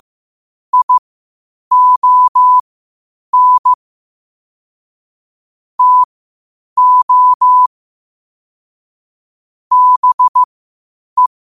Morse code, heard as a keyed tone, about 15 wpm, spaced as for 8 wpm, 1000 Hz. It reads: ION TO BE
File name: 8wpm